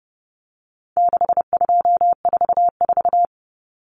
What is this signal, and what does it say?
Morse code, heard as a keyed tone, about 30 wpm, 700 Hz. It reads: 6244
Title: Numbers - 4-Digit 30wpm